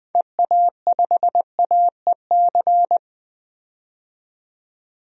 Morse code, heard as a keyed tone, about 21 wpm, 700 Hz. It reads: EA5AEC
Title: DXCC-100 Call Signs - 6 Characters - 20wpm